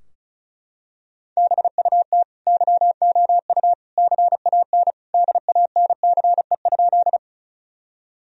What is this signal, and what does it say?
Morse code, heard as a keyed tone, about 34 wpm, 700 Hz. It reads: BUT YOU CAN DANCE?
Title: Sentences from Top 1000 Words - Review Part 4 - 35wpm